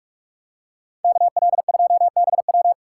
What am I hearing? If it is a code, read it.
Morse code, heard as a keyed tone, about 44 wpm, 700 Hz. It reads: KL2BW